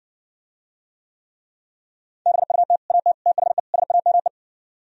Morse code, heard as a keyed tone, about 60 wpm, 700 Hz. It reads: BUT AT THE SAME